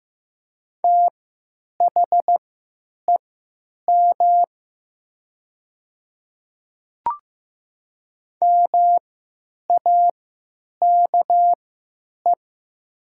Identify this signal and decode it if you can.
Morse code, heard as a keyed tone, about 15 wpm, spaced as for 9 wpm, 700 Hz. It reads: THEM MAKE
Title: 3x Character Spacing - Top 300 Words - 15wpm